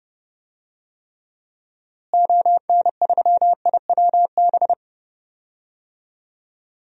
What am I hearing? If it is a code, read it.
Morse code, heard as a keyed tone, about 30 wpm, 700 Hz. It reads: ON3IWB
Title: DXCC-100 Call Signs - 6 Characters - Repeated 3x - 30wpm